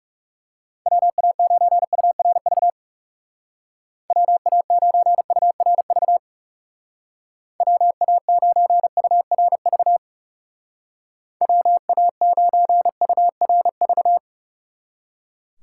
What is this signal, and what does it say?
Morse code, heard as a keyed tone, about 38 wpm, 700 Hz. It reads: WA9URV WA9URV WA9URV WA9URV